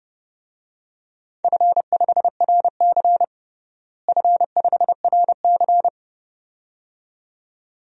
Morse code, heard as a keyed tone, about 30 wpm, 700 Hz. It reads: F5RC F5RC
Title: DXCC-100 Call Signs - 4 Characters - Repeated 2x - 30wpm